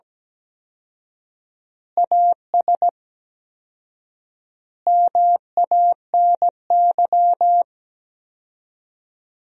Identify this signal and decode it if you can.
Morse code, heard as a keyed tone, about 17 wpm, 700 Hz. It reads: AS MANY
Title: 4x Word-Spacing - Sentences from Top 400 Words - 17wpm